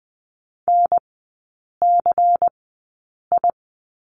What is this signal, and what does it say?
Morse code, heard as a keyed tone, about 20 wpm, spaced as for 8 wpm, 690 Hz. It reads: NCI
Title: 093 Lesson 16e - Letter c - Three Characters 20wpm